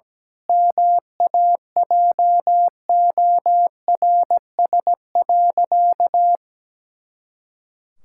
Morse code, heard as a keyed tone, about 17 wpm, 700 Hz. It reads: MAJORS.
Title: Pangram Challenge v2 - 17wpm